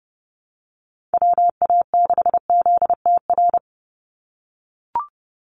Morse code, heard as a keyed tone, about 30 wpm, 700 Hz. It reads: WA6ZTR
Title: USA Call Signs 30wpm